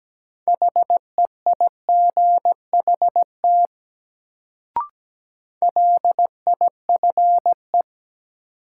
Morse code, heard as a keyed tone, about 17 wpm, 700 Hz. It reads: HEIGHT LIFE
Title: Sets of 5 Words - New Word Drill - 17wpm